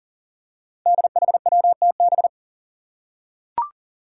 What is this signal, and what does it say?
Morse code, heard as a keyed tone, about 40 wpm, 700 Hz. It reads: DHWTB
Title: Five Challenging Letter-Number Combinations 40wpm